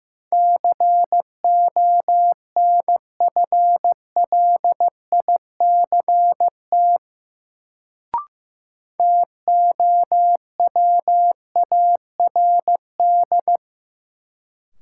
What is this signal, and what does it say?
Morse code, heard as a keyed tone, about 15 wpm, 700 Hz. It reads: CONFLICT TOWARD
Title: Sentences from Top 1300 Words - New Word Drill Alt - 15wpm